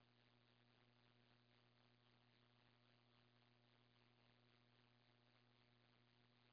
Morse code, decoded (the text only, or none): none